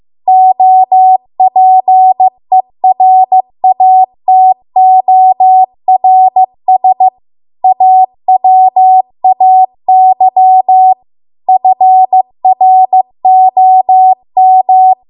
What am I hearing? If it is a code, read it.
Morse code, heard as a keyed tone, about 15 wpm, 750 Hz. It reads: OPERATORS AWAY FROM